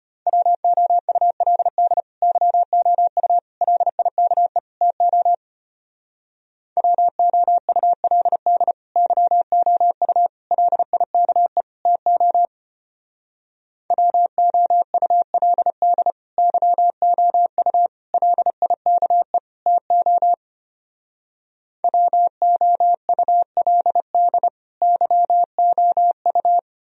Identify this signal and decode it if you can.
Morse code, heard as a keyed tone, about 32 wpm, 700 Hz. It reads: WOULD YOU LIKE TO WOULD YOU LIKE TO WOULD YOU LIKE TO WOULD YOU